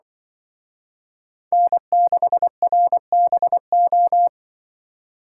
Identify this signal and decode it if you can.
Morse code, heard as a keyed tone, about 24 wpm, 700 Hz. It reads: N6RBO
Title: Speed Racing - USA Call Signs 20wpm